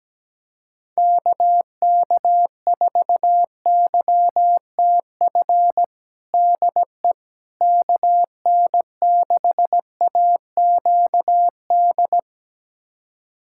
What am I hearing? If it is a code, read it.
Morse code, heard as a keyed tone, about 17 wpm, 700 Hz. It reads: KK4YTF DE KN6AQD
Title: USA Callsign DE Callsign - 6 Characters - 17wpm